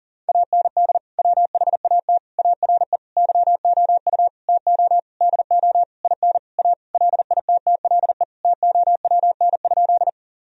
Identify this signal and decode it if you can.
Morse code, heard as a keyed tone, about 40 wpm, 700 Hz. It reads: AND WHAT ARE YOU TO DO IN A LITTLE TOWN?
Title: Sentences from Top 1000 Words - Review Part 2 - 40wpm